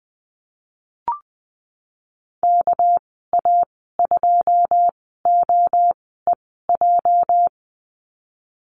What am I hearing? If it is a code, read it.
Morse code, heard as a keyed tone, about 20 wpm, spaced as for 15 wpm, 700 Hz. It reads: KA2OEJ
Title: 207 Lesson 33g - Letter j - Call Signs 20wpm